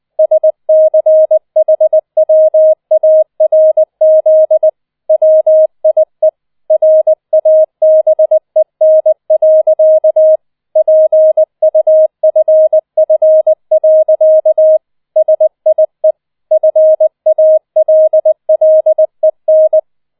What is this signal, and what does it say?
Morse code, heard as a keyed tone, about 20 wpm, 630 Hz. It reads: SCHWARZ WIE RABEN. PUFF. SIE FALLEN